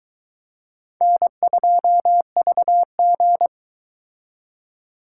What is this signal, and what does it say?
Morse code, heard as a keyed tone, about 23 wpm, 700 Hz. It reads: N2VG